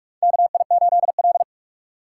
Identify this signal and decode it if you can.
Morse code, heard as a keyed tone, about 46 wpm, 700 Hz. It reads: KI8L
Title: USA Call Signs - 6x Character Spacing - 45wpm